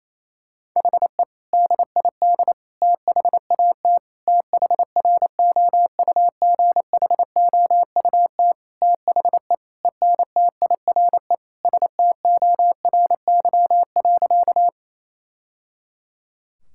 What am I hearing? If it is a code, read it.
Morse code, heard as a keyed tone, about 28 wpm, 700 Hz. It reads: HE DID THAT THROUGHOUT THE ENTIRE STORY.
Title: Sentences from Top 1000 Words - Review Part 5 - 28wpm